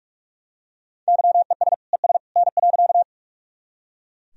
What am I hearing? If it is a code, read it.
Morse code, heard as a keyed tone, about 43 wpm, 700 Hz. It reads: YES ES N.
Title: Sentences from Top 500 Words and Abbreviations 45wpm